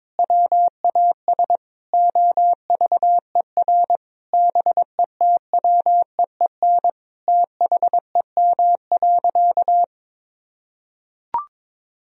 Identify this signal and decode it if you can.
Morse code, heard as a keyed tone, about 22 wpm, 700 Hz. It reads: WAS OVER BETWEEN THEM.
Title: Sentences from Top 1000 Words - Review Part 1 - 22wpm